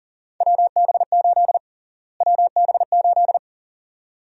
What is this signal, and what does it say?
Morse code, heard as a keyed tone, about 40 wpm, 700 Hz. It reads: WB8 WB8
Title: USA Call Sign Prefixes - Repeated 2x - 40wpm